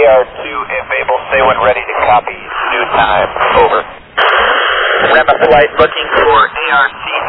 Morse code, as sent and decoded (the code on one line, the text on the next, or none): none